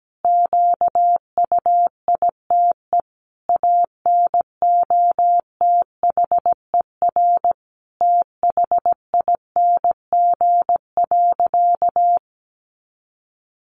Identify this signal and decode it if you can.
Morse code, heard as a keyed tone, about 17 wpm, 700 Hz. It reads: QUITE ANOTHER THING.
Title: Sentences from Top 700 Words 17wpm